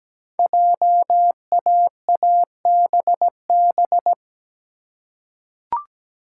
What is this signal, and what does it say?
Morse code, heard as a keyed tone, about 17 wpm, 700 Hz. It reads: JAABB
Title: Code Groups - 5 Letters - 17wpm